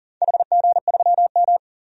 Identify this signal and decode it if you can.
Morse code, heard as a keyed tone, about 40 wpm, 700 Hz. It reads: HG3M
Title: DXCC-100 Call Signs - 4 Characters - 40wpm